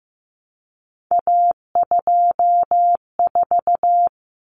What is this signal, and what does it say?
Morse code, heard as a keyed tone, about 15 wpm, 700 Hz. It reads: A24
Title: Common Cut Numbers - 3-Digit 15wpm